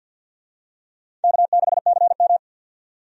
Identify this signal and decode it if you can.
Morse code, heard as a keyed tone, about 51 wpm, 700 Hz. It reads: K6CM
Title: USA Call Signs - 4 Characters - 50wpm